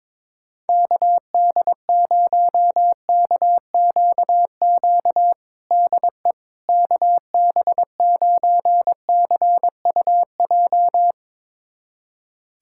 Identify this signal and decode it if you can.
Morse code, heard as a keyed tone, about 22 wpm, 700 Hz. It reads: KD0KQQ DE KB9CUJ
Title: USA Callsign DE Callsign - 6 Characters - 22wpm